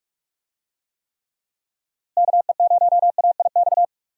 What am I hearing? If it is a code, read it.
Morse code, heard as a keyed tone, about 46 wpm, 700 Hz. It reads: KE0AIX
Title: USA Call Signs - 6 Characters - Repeated 2x - 45wpm